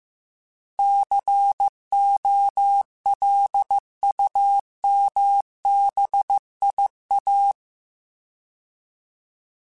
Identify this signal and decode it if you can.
Morse code, heard as a keyed tone, about 15 wpm, 800 Hz. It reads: COLUMBIA